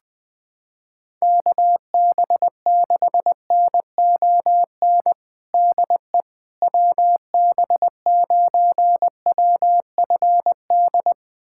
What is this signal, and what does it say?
Morse code, heard as a keyed tone, about 20 wpm, 700 Hz. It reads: KB6NON DE WB9WFD
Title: USA Callsign DE Callsign 20wpm